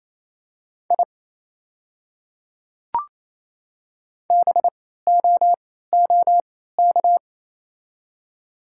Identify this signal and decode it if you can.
Morse code, heard as a keyed tone, about 28 wpm, spaced as for 16 wpm, 700 Hz. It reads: I BOOK
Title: 3x Character Spacing - Top 300 Words - 28wpm